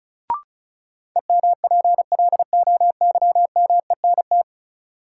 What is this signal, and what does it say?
Morse code, heard as a keyed tone, about 36 wpm, 700 Hz. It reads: EMPLOYMENT
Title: Sentences from Top 1500 Words - New Word Drill Alt - 35wpm